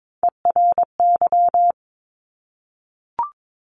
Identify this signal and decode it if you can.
Morse code, heard as a keyed tone, about 22 wpm, 700 Hz. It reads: ERY
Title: Top Three Letter Combinations in Words 22wpm